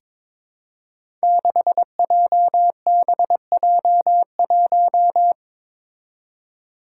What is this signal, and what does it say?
Morse code, heard as a keyed tone, about 22 wpm, 700 Hz. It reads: 6JBJ1
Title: Five Challenging Letter-Number Combinations 22wpm